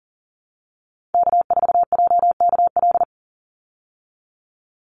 Morse code, heard as a keyed tone, about 40 wpm, 700 Hz. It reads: K4JKL
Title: Call Signs Repeated 2x - 40wpm